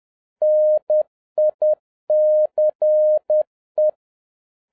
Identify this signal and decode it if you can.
Morse code, heard as a keyed tone, about 10 wpm, 610 Hz. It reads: NICE
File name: 2011-07-mp3 / n56 BACK FROM A BIG ACCIDENT 10wpm